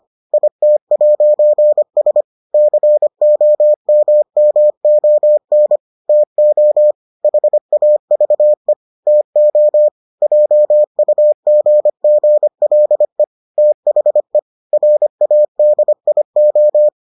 Morse code, heard as a keyed tone, about 25 wpm, 600 Hz. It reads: IT'S COMMON TO HAVE TO JUGGLE THE RADIO